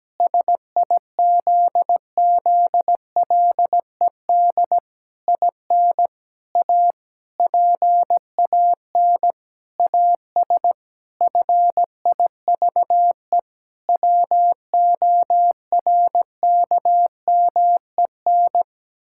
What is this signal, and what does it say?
Morse code, heard as a keyed tone, about 17 wpm, 700 Hz. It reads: SIZZLED IN A PAN AS FIVE WORKMEN
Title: Pangram Challenge v2 - 17wpm